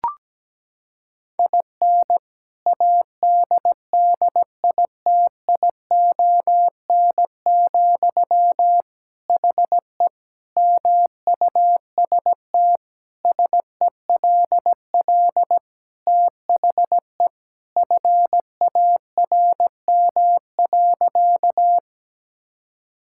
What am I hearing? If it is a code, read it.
Morse code, heard as a keyed tone, about 17 wpm, 700 Hz. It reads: IN ADDITION, HE MUST SELL THE FARM.
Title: Sentences from Top 1000 Words - Review Part 6 - 17wpm